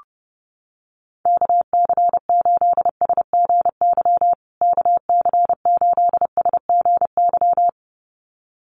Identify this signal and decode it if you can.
Morse code, heard as a keyed tone, about 28 wpm, 700 Hz. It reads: KC8SGY KC8SGY
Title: USA Call Signs Repeated 2x 30wpm